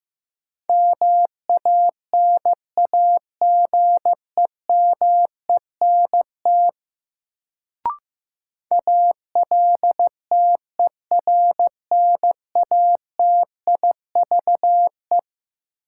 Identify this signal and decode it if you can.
Morse code, heard as a keyed tone, about 15 wpm, 700 Hz. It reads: MANAGEMENT ALTERNATIVE
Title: Sentences from Top 1000 Words - Review Part 4 - New Word Drill - 15wpm